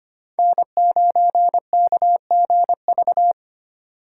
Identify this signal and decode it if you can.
Morse code, heard as a keyed tone, about 32 wpm, 700 Hz. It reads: N9KGV